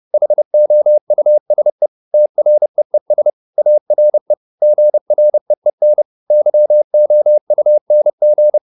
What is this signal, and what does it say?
Morse code, heard as a keyed tone, about 30 wpm, 600 Hz. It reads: HOUSE TREES ARE GREEN YOUNG